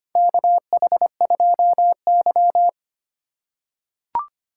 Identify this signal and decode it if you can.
Morse code, heard as a keyed tone, about 25 wpm, 700 Hz. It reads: KH2Y
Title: USA Call Signs - 4 Characters - 2x Character Spacing - 25wpm